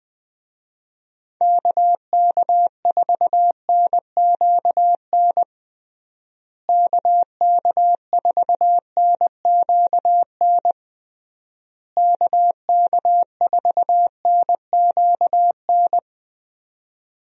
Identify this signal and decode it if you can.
Morse code, heard as a keyed tone, about 20 wpm, 700 Hz. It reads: KK4NQN KK4NQN KK4NQN
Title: USA Call Signs - 6 Characters - Repeated 3x - 20wpm